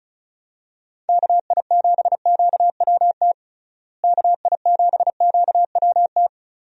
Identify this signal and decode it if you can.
Morse code, heard as a keyed tone, about 36 wpm, 700 Hz. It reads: KI7QWT KI7QWT
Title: USA Call Signs - 6 Characters - Repeated 2x - 35wpm